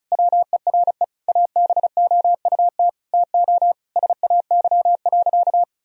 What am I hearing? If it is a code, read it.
Morse code, heard as a keyed tone, about 34 wpm, 700 Hz. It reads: WERE ABOUT TO SAY.
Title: Sentences from Top 1000 Words - Review Part 2 - 35wpm